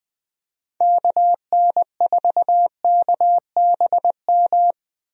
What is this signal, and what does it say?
Morse code, heard as a keyed tone, about 21 wpm, 700 Hz. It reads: KN4KBM
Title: USA Call Signs - 6 Characters - 20wpm